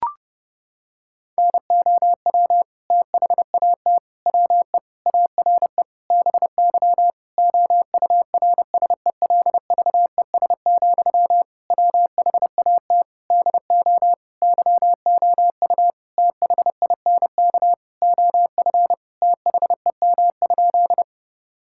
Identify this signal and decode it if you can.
Morse code, heard as a keyed tone, about 30 wpm, 700 Hz. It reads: NOW THAT WE ARE BY OURSELVES, WHAT DO YOU THINK OF THEM?
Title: Sentences from Top 1600 Words 30wpm